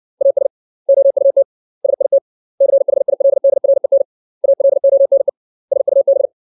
Morse code, heard as a keyed tone, about 64 wpm, 550 Hz. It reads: AS OUT SIT CHILDREN ALONE SUB
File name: 632 words / 632 Words 60WPM